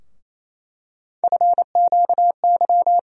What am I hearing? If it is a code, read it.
Morse code, heard as a keyed tone, about 28 wpm, 700 Hz. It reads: FQY
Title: USA Call Sign Postfixes - 3 Letters 28wpm